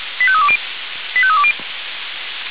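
Five-tone selective call (ZVEI), heard as quick runs of five tones, tones of about 70 ms. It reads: 85320 85320
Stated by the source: ZVEI five-tone selective call received on SDR.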